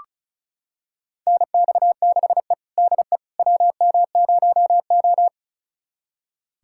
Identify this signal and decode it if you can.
Morse code, heard as a keyed tone, about 36 wpm, 700 Hz. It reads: NX6E DE WM0O
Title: USA Callsign DE Callsign - 4 Characters - 35wpm